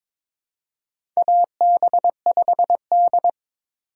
Morse code, heard as a keyed tone, about 22 wpm, 700 Hz. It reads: AB5D
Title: USA Call Signs - 4 Characters - 4x Character Spacing - 22wpm